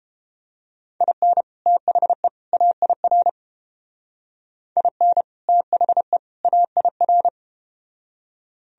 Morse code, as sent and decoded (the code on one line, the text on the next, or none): .. -. / - .... / .- .. .-. / .. -. / - .... / .- .. .-.
IN THE AIR IN THE AIR